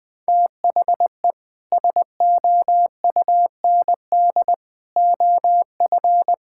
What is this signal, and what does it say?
Morse code, heard as a keyed tone, about 19 wpm, 700 Hz. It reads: THE SOUND OF
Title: Sentences with 3-Word Phrases - 20wpm